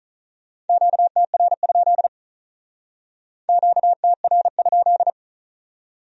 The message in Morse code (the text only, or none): QTR? QTR?